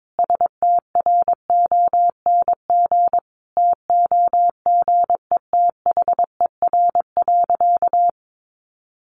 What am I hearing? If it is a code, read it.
Morse code, heard as a keyed tone, about 22 wpm, 700 Hz. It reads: STRONG TOGETHER.
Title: Sentences from Top 500 Words 22wpm